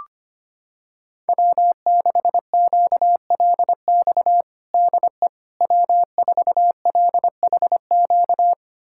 Morse code, heard as a keyed tone, about 25 wpm, 700 Hz. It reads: W6QLX DE W4LHQ
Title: USA Callsign DE Callsign - 5 Characters - 25wpm